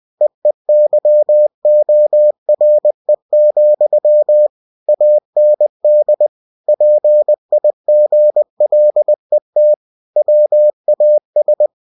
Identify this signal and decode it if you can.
Morse code, heard as a keyed tone, about 20 wpm, 600 Hz. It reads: EEYORE, AND PIGLET WAS
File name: cw-books / winnie-the-pooh / winnie-20wpm-page-0057